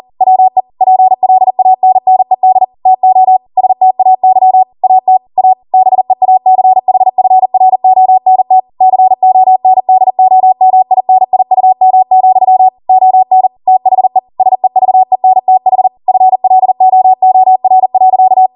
Morse code, heard as a keyed tone, about 40 wpm, 750 Hz. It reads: WE PLANNED TO STAY AT A BEACHFRONT CONDOMINIUM, ON THE SEVENTH FLOOR.